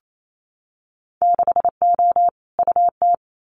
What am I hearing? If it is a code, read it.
Morse code, heard as a keyed tone, about 28 wpm, 700 Hz. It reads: 6O UT
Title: Winter Field Day Exchanges Encore 28wpm